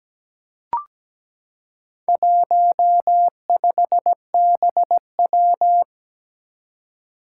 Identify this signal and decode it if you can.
Morse code, heard as a keyed tone, about 17 wpm, 700 Hz. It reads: 15BW
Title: Four Challenging Letter-Number Combinations 17wpm